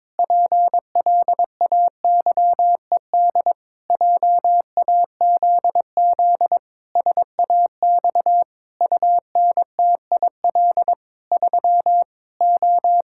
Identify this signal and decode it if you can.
Morse code, heard as a keyed tone, about 22 wpm, 700 Hz. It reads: PLAYED JAZZ SAX UNTIL 3 O
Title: Pangram Challenge v2 - 22wpm